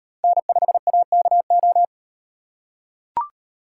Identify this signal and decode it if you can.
Morse code, heard as a keyed tone, about 38 wpm, 700 Hz. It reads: N5AKO